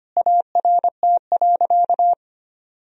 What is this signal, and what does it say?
Morse code, heard as a keyed tone, about 25 wpm, 700 Hz. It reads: ART.
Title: Sentences from Top 1000 Words - Review Part 3 - 25wpm